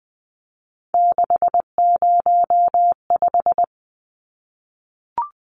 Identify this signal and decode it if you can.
Morse code, heard as a keyed tone, about 20 wpm, 700 Hz. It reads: 605
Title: Numbers - 3-Digit 20wpm